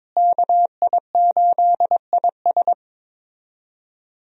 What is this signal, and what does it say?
Morse code, heard as a keyed tone, about 21 wpm, 700 Hz. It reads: KI8IS
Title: USA Call Signs - 5 Characters - Repeated 3x - 22wpm